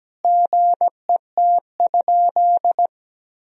Morse code, heard as a keyed tone, about 17 wpm, 700 Hz. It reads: GET?
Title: Sentences from Top 1000 Words - Review Part 3 - 17wpm